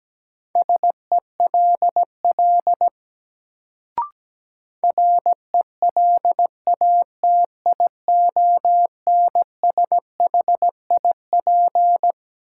Morse code, heard as a keyed tone, about 17 wpm, 700 Hz. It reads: SELL RELATIONSHIP
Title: Sentences from Top 1100 Words - New Word Drill Alt - 17wpm